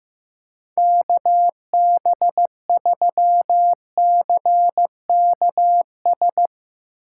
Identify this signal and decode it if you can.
Morse code, heard as a keyed tone, about 15 wpm, 700 Hz. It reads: KB3CKS
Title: Speed Racing - USA 2x3 Call Signs 15wpm